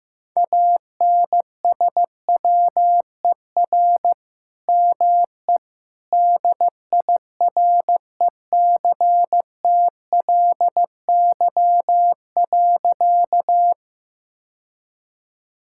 Morse code, heard as a keyed tone, about 15 wpm, 700 Hz. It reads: ANSWER ME DIRECTLY.
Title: Sentences from Top 1000 Words - Review Part 5 - 15wpm